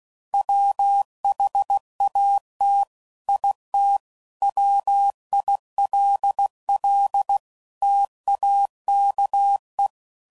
Morse code, heard as a keyed tone, about 16 wpm, 800 Hz. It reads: WHAT IT WILL TAKE
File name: CW Code Groups @15WPM / Technical Subjects / OR Gate